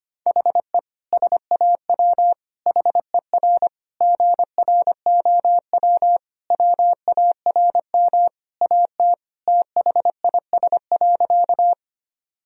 Morse code, heard as a keyed tone, about 25 wpm, 700 Hz. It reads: HE SAW HER GROW WARM AT THIS.
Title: Sentences from Top 1000 Words - Review Part 2 - 25wpm